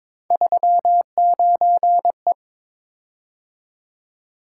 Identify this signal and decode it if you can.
Morse code, heard as a keyed tone, about 22 wpm, 700 Hz. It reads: 39E
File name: practice / 22wpm / Three Letter-Number Combinations - 22wpm